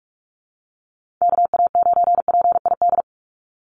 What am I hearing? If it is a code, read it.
Morse code, heard as a keyed tone, about 44 wpm, 700 Hz. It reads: KA9PID
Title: USA Call Signs 45wpm